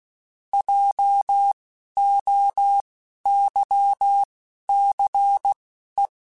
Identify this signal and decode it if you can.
Morse code, heard as a keyed tone, about 16 wpm, spaced as for 11 wpm, 800 Hz. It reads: JOYCE
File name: Alphabetical / Alphabetical mp3s / Js @10